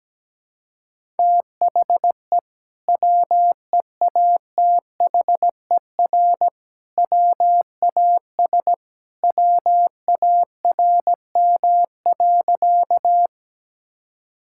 Morse code, heard as a keyed tone, about 17 wpm, 700 Hz. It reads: THE WEATHER WAS WARM.
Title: Sentences from Top 1000 Words - Review Part 5 - 17wpm